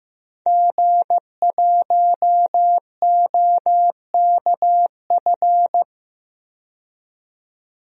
Morse code, heard as a keyed tone, about 15 wpm, 700 Hz. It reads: G1OKF
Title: DXCC-100 Call Signs - 5 Characters - 15wpm